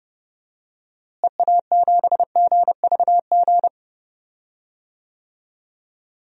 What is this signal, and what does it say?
Morse code, heard as a keyed tone, about 32 wpm, 700 Hz. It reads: EA7GVG
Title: DXCC-100 Call Signs - 6 Characters - Repeated 3x - 30wpm